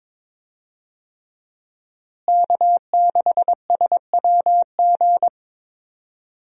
Morse code, heard as a keyed tone, about 22 wpm, 700 Hz. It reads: K6SWG
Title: USA Call Signs - 5 Characters - Repeated 3x - 22wpm